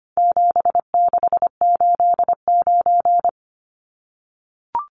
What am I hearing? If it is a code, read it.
Morse code, heard as a keyed tone, about 25 wpm, 700 Hz. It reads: 7689
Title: Numbers - 4-Digit 25wpm